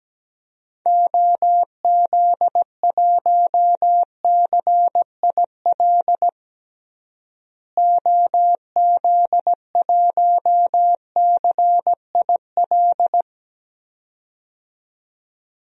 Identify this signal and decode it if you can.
Morse code, heard as a keyed tone, about 17 wpm, 700 Hz. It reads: OZ1CIL OZ1CIL
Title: DXCC-100 Call Signs - 6 Characters - Repeated 2x - 17wpm